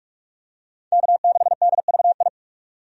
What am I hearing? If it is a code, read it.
Morse code, heard as a keyed tone, about 47 wpm, 700 Hz. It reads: K6DVI
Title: USA Call Signs - 5 Characters - 45wpm